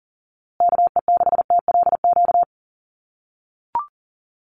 Morse code, heard as a keyed tone, about 39 wpm, 700 Hz. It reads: KE6TLQ
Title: USA Call Signs Repeated 2x 40wpm